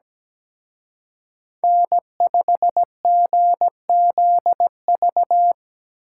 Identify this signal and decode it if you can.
Morse code, heard as a keyed tone, about 20 wpm, 700 Hz. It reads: N5GZV
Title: Speed Racing - USA Call Signs 17wpm